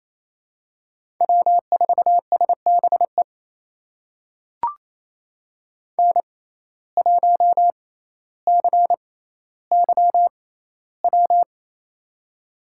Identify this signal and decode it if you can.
Morse code, heard as a keyed tone, about 28 wpm, spaced as for 14 wpm, 700 Hz. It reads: W4SBE N1CYW
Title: USA Call Signs - 5 Characters - 6x Character Spacing - 28wpm